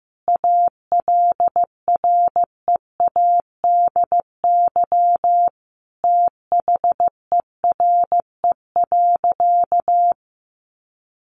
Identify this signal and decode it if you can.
Morse code, heard as a keyed tone, about 15 wpm, 700 Hz. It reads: ALREADY THERE.
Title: Sentences from Top 700 Words 15wpm